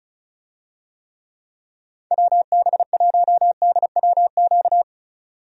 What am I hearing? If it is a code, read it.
Morse code, heard as a keyed tone, about 34 wpm, 700 Hz. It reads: WB1DWQ
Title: USA Call Signs - 6 Characters - 35wpm